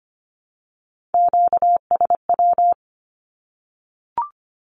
Morse code, heard as a keyed tone, about 25 wpm, 700 Hz. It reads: QSW
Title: Q-Codes 25wpm